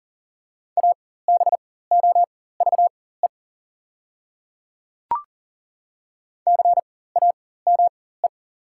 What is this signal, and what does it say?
Morse code, heard as a keyed tone, about 40 wpm, spaced as for 19 wpm, 700 Hz. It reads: ABOVE CAME
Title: 4x Character Spacing - Top 500 Words - 40wpm